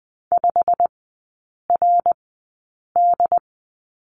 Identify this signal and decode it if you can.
Morse code, heard as a keyed tone, about 20 wpm, 700 Hz. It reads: 5 R D
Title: 107 Lesson 18e - Letter m - Three Characters 20wpm